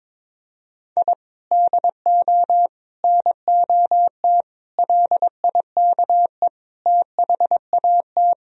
Morse code, heard as a keyed tone, about 22 wpm, 700 Hz. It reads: I DO NOT LIKE THAT
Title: Sentences from Top 1000 Words - Review Part 2 - 22wpm